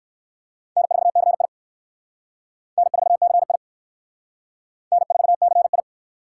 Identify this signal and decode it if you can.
Morse code, heard as a keyed tone, about 55 wpm, 700 Hz. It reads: N4CI N4CI N4CI
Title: Speed Racing - USA 1x2 Call Signs 45wpm